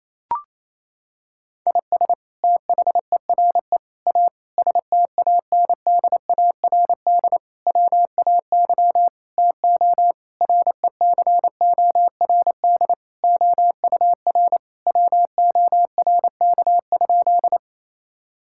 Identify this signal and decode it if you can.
Morse code, heard as a keyed tone, about 28 wpm, 700 Hz. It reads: IS THERE A STANDARD WAY TO RECORD OUR WORK?
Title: Sentences from Top 1000 Words - Review Part 6 - 28wpm